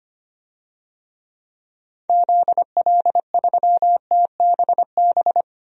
Morse code, heard as a keyed tone, about 25 wpm, 700 Hz. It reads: ZL3TBB
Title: DXCC-100 Call Signs - 6 Characters - 25wpm